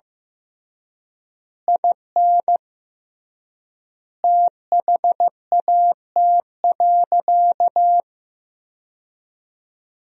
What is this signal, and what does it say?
Morse code, heard as a keyed tone, about 16 wpm, 700 Hz. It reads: IN THAT.
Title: 3x Word-Spacing - Sentences from Top 200 Words - 15wpm